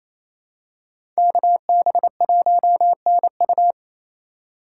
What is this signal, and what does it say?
Morse code, heard as a keyed tone, about 28 wpm, 700 Hz. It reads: KB1NU
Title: USA Call Signs - 3x Character Spacing - 28wpm